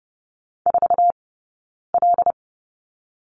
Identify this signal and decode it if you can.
Morse code, heard as a keyed tone, about 32 wpm, 700 Hz. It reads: 4 L
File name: Lesson 19d - Letter w - Two Characters 30wpm